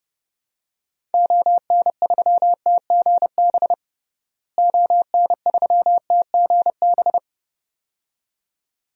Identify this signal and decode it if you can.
Morse code, heard as a keyed tone, about 30 wpm, 700 Hz. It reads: ON3TGB ON3TGB
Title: DXCC-100 Call Signs - 6 Characters - Repeated 2x - 30wpm